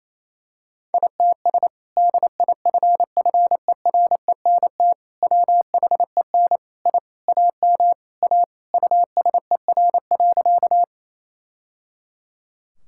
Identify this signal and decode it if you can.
Morse code, heard as a keyed tone, about 27 wpm, 700 Hz. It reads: ITS DIFFERENT WHEN I AM A USER.
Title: Sentences from Top 1000 Words - Review Part 5 - 28wpm